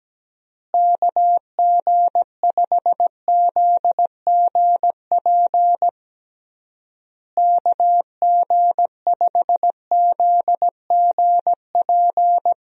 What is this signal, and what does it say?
Morse code, heard as a keyed tone, about 17 wpm, 700 Hz. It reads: KG5ZGP KG5ZGP
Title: USA Call Signs - 6 Characters - Repeated 3x - 17wpm